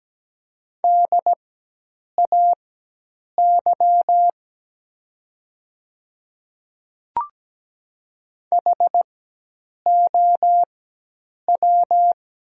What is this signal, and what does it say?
Morse code, heard as a keyed tone, about 17 wpm, spaced as for 8 wpm, 700 Hz. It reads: DAY HOW
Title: 4x Character Spacing - Top 300 Words - 17wpm